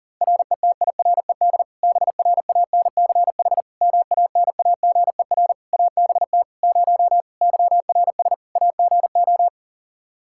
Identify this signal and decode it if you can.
Morse code, heard as a keyed tone, about 40 wpm, 700 Hz. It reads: RETIRED BRANCH MANAGER ABT 0 YRS AGO
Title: Practice QSO - Exchange 3 - More Info - 40wpm